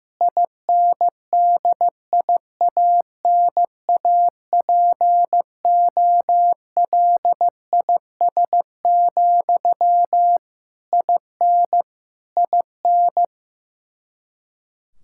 Morse code, heard as a keyed tone, about 15 wpm, 700 Hz. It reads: INDIANAPOLIS, IN IN
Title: Practice QSO - Exchange 1 - Beginning - 15wpm